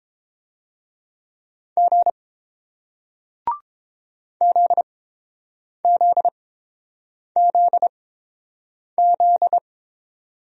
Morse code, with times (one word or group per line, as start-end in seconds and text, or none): 1.8-2.1 G
4.4-9.6 ZZZZ